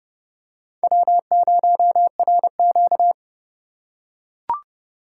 Morse code, heard as a keyed tone, about 30 wpm, 700 Hz. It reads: W0RQ
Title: USA Call Signs - 4 Characters - 2x Character Spacing - 30wpm